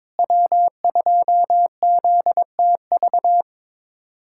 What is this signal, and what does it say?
Morse code, heard as a keyed tone, about 22 wpm, 700 Hz. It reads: W2ZTV
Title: USA Callsign DE Callsign 22wpm